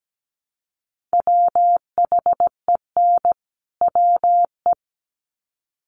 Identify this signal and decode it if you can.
Morse code, heard as a keyed tone, about 17 wpm, 700 Hz. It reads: WHEN WE
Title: Sets of 2 Words 17wpm